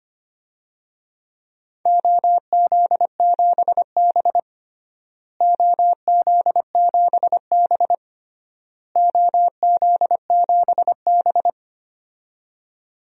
Morse code, heard as a keyed tone, about 25 wpm, 700 Hz. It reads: OZ7B OZ7B OZ7B
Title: DXCC-100 Call Signs - 4 Characters - Repeated 3x - 25wpm